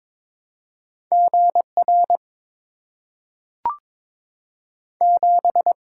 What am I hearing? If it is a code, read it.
Morse code, heard as a keyed tone, about 22 wpm, 700 Hz. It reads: G R 7
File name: practice / 22wpm / Two Letter-Number Combinations - 22wpm